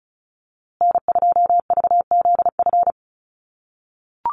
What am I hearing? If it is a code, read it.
Morse code, heard as a keyed tone, about 34 wpm, 700 Hz. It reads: N2VZF
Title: USA Call Signs Repeated 3x 35wpm